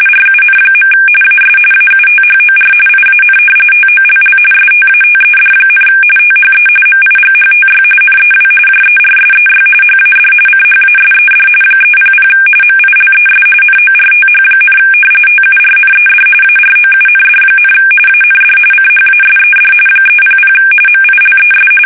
RTTY, about 45.45 baud. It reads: RUN. AS AUTHOR IAN TOLL STATED, THE MIGHTY NEW JERSEY HAD CHASED 300 MILES NORTH, TURNED AROUND, AND CHASED 300 MILES
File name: fsk850 / battleshipnj